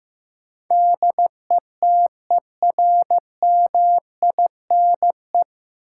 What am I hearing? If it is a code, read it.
Morse code, heard as a keyed tone, about 16 wpm, 700 Hz. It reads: DETERMINE
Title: Sentences from Top 700 Words - New Word Drill - 15wpm